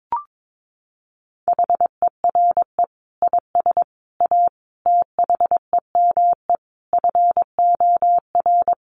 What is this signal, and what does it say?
Morse code, heard as a keyed tone, about 22 wpm, 700 Hz. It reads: HERE IS A THEME FOR
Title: Sentences from Top 1100 Words 22wpm